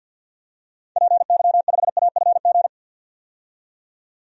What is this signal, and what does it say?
Morse code, heard as a keyed tone, about 51 wpm, 700 Hz. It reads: PY5AFG
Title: DXCC-100 Call Signs - 6 Characters - 50wpm